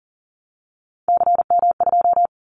Morse code, heard as a keyed tone, about 38 wpm, 700 Hz. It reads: CM2
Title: Callsign Prefixes for Top 100 DXCC - 40wpm